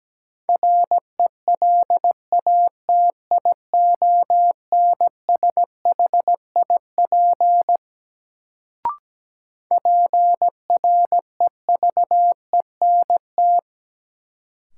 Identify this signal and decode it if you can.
Morse code, heard as a keyed tone, about 17 wpm, 700 Hz. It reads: RELATIONSHIP PREVENT
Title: Sentences from Top 1100 Words - New Word Drill - 17wpm